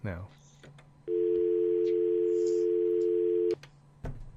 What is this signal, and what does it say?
Dial tone (North America), 350 + 440 Hz, with speech.